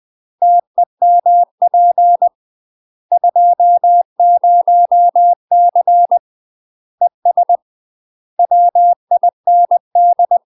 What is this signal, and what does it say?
Morse code, heard as a keyed tone, about 20 wpm, 700 Hz. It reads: TEMP 20C ES WIND
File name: UK MASTER QSOs TRAINING / 20WPM / UK MASTER QSOs 0026